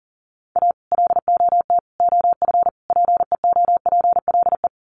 Morse code, heard as a keyed tone, about 40 wpm, 700 Hz. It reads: A LOT OF PEOPLE
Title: Sets of 4 Words 40wpm